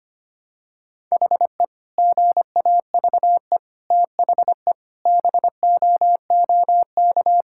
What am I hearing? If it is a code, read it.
Morse code, heard as a keyed tone, about 25 wpm, 700 Hz. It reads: HE GAVE THE BOOK
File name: practice / 25wpm / Sentences with 4-Word Phrases - 25wpm